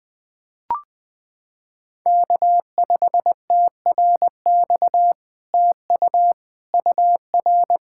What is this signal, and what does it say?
Morse code, heard as a keyed tone, about 20 wpm, 700 Hz. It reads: K5TRX TU UR
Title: SOTA QSO Practice 20wpm